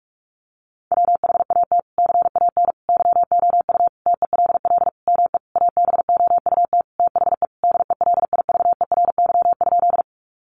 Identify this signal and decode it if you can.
Morse code, heard as a keyed tone, about 44 wpm, 700 Hz. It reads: WHAT CAN YOU TELL ME ABOUT THE DELIVERY?
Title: Sentences from Top 1500 Words 45wpm